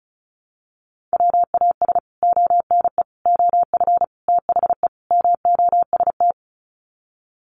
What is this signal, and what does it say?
Morse code, heard as a keyed tone, about 34 wpm, 700 Hz. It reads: WAS ONE OF THE MOST
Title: Sets of 5 Words 35wpm